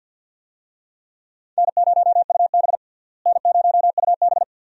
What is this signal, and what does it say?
Morse code, heard as a keyed tone, about 48 wpm, 700 Hz. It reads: N0UB N0UB
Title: USA Call Signs - 4 Characters - Repeated 2x - 50wpm